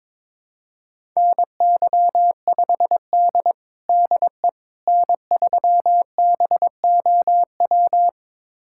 Morse code, heard as a keyed tone, about 21 wpm, 700 Hz. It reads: NY5D DE N3BOW
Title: USA Callsign DE Callsign 22wpm